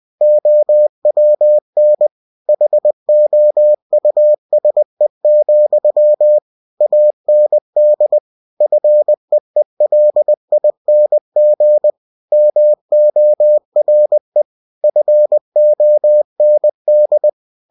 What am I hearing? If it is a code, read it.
Morse code, heard as a keyed tone, about 20 wpm, 600 Hz. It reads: OWN HOUSE, AND FEELING MORE FOND
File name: cw-books / winnie-the-pooh / winnie-20wpm-page-0071